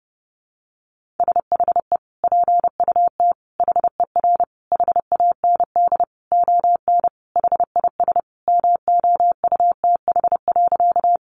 Morse code, heard as a keyed tone, about 30 wpm, 700 Hz. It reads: SHE PUT HER HAND ON HIS MOUTH.
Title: Sentences from Top 1500 Words 30wpm